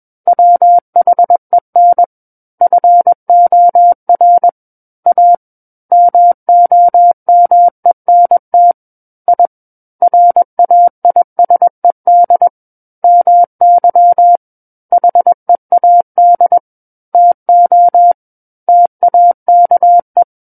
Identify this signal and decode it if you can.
Morse code, heard as a keyed tone, about 21 wpm, 700 Hz. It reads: WHEN FOR A MOMENT I RAISED MY HEAD TO TAKE
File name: Book One / 12-Chapter 12 - 21 WPM